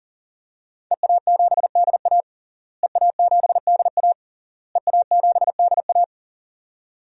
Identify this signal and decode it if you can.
Morse code, heard as a keyed tone, about 40 wpm, 700 Hz. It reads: EA7DA EA7DA EA7DA
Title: DXCC-100 Call Signs - 5 Characters - Repeated 3x - 40wpm